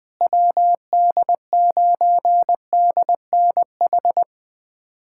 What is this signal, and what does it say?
Morse code, heard as a keyed tone, about 21 wpm, 700 Hz. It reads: WD9DNH
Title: USA Callsign DE Callsign - 6 Characters - 20wpm